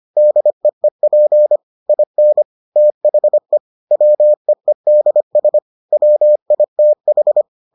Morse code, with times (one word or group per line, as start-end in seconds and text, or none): 0.2-1.6 DEEP
1.9-2.4 IN
2.8-3.6 THE
3.9-5.6 WEEDS
5.9-7.4 WITH